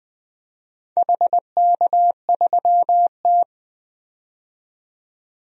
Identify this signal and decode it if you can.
Morse code, heard as a keyed tone, about 20 wpm, 700 Hz. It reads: HK3T
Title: DXCC-100 Call Signs - 4 Characters - 20wpm